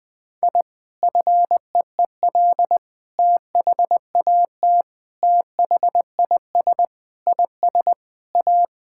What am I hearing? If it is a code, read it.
Morse code, heard as a keyed tone, about 20 wpm, 700 Hz. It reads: I FEEL THAT THIS IS A
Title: Sentences with 4-Word Phrases - Spoken Beginning - 20wpm